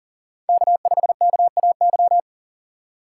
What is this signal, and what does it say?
Morse code, heard as a keyed tone, about 40 wpm, 700 Hz. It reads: K5KAY
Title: USA Call Signs - 5 Characters - 4x Character Spacing - 40wpm